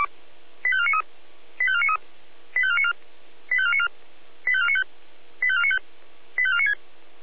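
Five-tone selective call (ZVEI), heard as quick runs of five tones, tones of about 70 ms. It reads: 85493 85493 85494 85494 85495 85495 85496
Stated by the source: ZVEI five-tone selective call 85491 received on SDR.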